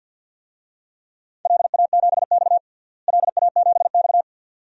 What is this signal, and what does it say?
Morse code, heard as a keyed tone, about 49 wpm, 700 Hz. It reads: LA7X LA7X
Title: DXCC-100 Call Signs - 4 Characters - Repeated 2x - 50wpm